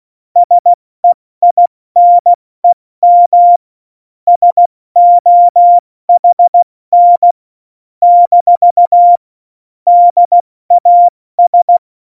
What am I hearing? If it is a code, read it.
Morse code, heard as a keyed tone, about 16 wpm, 700 Hz. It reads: SEINEM SOHN - DAS
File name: Morsekurs / Witze / Witz 11. Juni 2019, 70 F80